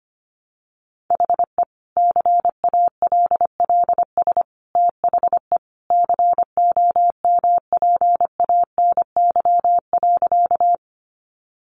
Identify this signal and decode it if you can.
Morse code, heard as a keyed tone, about 25 wpm, 700 Hz. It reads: HE CALLS THE COMPANY.
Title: Additional Easy Phrases 25wpm